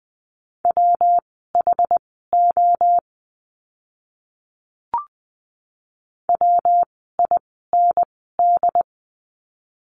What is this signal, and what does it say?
Morse code, heard as a keyed tone, about 20 wpm, spaced as for 14 wpm, 700 Hz. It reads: WHO WIND
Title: 115 Lesson 19f - Letter w - Words 20wpm